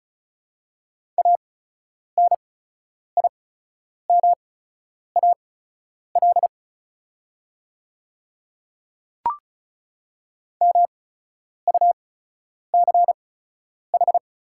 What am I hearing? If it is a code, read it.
Morse code, heard as a keyed tone, about 36 wpm, spaced as for 10 wpm, 700 Hz. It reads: ANIMAL MUCH
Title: 8x Character Spacing - Top 200 Words - 35wpm